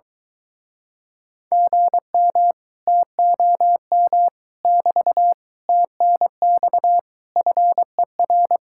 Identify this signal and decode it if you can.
Morse code, heard as a keyed tone, about 23 wpm, 700 Hz. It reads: GM TOM <BT> TNX FER